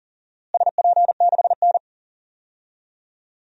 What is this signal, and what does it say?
Morse code, heard as a keyed tone, about 40 wpm, 700 Hz. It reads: SP6N
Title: DXCC-100 Call Signs - 4 Characters - Repeated 3x - 40wpm